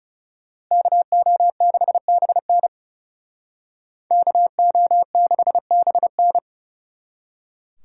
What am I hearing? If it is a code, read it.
Morse code, heard as a keyed tone, about 37 wpm, 700 Hz. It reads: KO6BN KO6BN
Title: Speed Racing - USA 2x2 Call Signs 30wpm